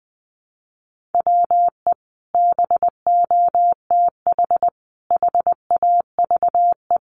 Morse code, heard as a keyed tone, about 20 wpm, 700 Hz. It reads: WE BOTH HAVE
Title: Sentences from Top 1200 Words 20wpm